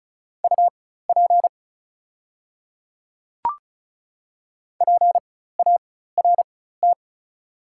Morse code, heard as a keyed tone, about 35 wpm, spaced as for 16 wpm, 690 Hz. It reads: UP PART
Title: 4x Character Spacing - Top 200 Words - 35wpm